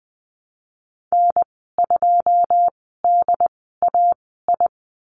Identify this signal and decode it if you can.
Morse code, heard as a keyed tone, about 20 wpm, spaced as for 15 wpm, 700 Hz. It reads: N2DAI
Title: 088 Lesson 15g - Number 5 - Call Signs 20wpm